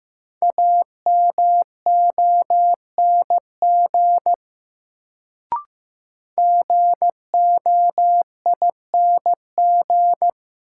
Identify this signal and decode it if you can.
Morse code, heard as a keyed tone, about 15 wpm, 700 Hz. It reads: AMONG GOING
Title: Sentences from Top 500 Words - New Word Drill - 15wpm